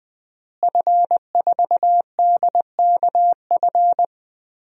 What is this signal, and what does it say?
Morse code, heard as a keyed tone, about 20 wpm, 700 Hz. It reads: F4DKF
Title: DXCC-100 Call Signs - 5 Characters - Repeated 2x - 20wpm